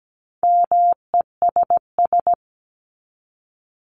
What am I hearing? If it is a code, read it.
Morse code, heard as a keyed tone, about 17 wpm, 700 Hz. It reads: MESS
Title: Top 1751-2000 Words 17wpm